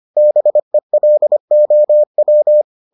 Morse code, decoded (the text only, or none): BELOW